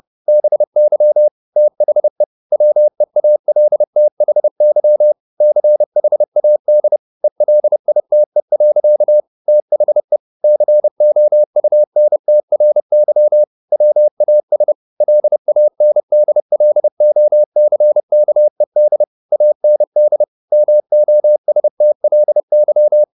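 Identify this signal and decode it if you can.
Morse code, heard as a keyed tone, about 30 wpm, 600 Hz. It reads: BY THE WEALTHY CHAD ELITE. THE COUNTRY WAS LANDLOCKED AND MOSTLY
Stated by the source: Morse code at 30 wpm, the novel Never by Ken Follett in English, from CW med Gnister.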